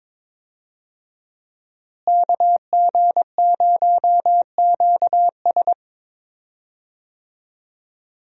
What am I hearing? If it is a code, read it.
Morse code, heard as a keyed tone, about 22 wpm, 700 Hz. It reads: KG0QS
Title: USA Call Signs - 5 Characters - Repeated 2x - 22wpm